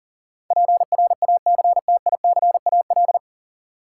Morse code, heard as a keyed tone, about 40 wpm, 700 Hz. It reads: PRACTICAL
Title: Sentences from Top 1400 Words - New Word Drill - 40wpm